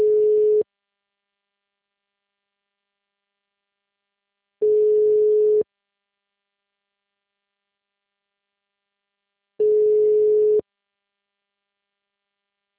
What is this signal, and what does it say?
Ringback tone (Europe, 425 Hz).
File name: out- -2030- - - .16213